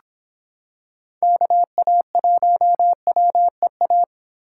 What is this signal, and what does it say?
Morse code, heard as a keyed tone, about 24 wpm, 700 Hz. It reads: KA1WEA